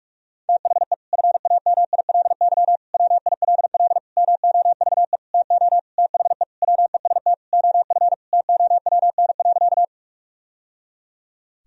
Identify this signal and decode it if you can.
Morse code, heard as a keyed tone, about 46 wpm, 700 Hz. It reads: THE FAMILY WILL MOVE TO THE WEST OF TOWN.
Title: Sentences from Top 1000 Words - Review Part 2 - 45wpm